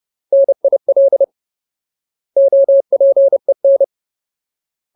Morse code, heard as a keyed tone, about 30 wpm, 550 Hz. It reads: NIL OPEN
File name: Ultimate Word List / 30WPM Ultimate